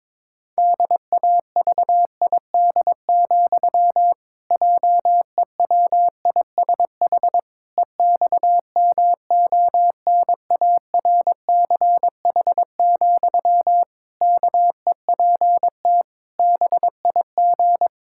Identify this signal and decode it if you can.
Morse code, heard as a keyed tone, about 22 wpm, 700 Hz. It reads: DAVID, JEWISH EXMONARCH, KEPT BIG
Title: Pangram Challenge v2 - 22wpm